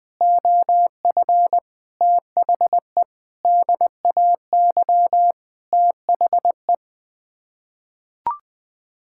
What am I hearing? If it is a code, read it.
Morse code, heard as a keyed tone, about 21 wpm, 700 Hz. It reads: OF THE DAY THE
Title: Sets of 8 Words - 20wpm